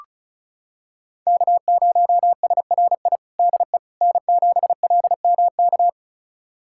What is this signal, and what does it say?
Morse code, heard as a keyed tone, about 34 wpm, 700 Hz. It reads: K0SRI DE N7LMK
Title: USA Callsign DE Callsign - 5 Characters - 35wpm